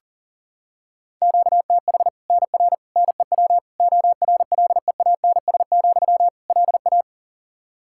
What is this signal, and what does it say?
Morse code, heard as a keyed tone, about 40 wpm, 700 Hz. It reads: QTH NR NEW ORLEANS, LA
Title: Practice QSO - Exchange 1 - Beginning - 40wpm